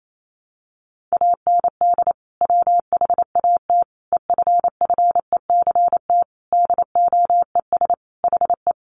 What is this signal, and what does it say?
Morse code, heard as a keyed tone, about 28 wpm, 700 Hz. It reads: AND WHAT EFFECT DOES HE
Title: Sentences from Top 1000 Words 28wpm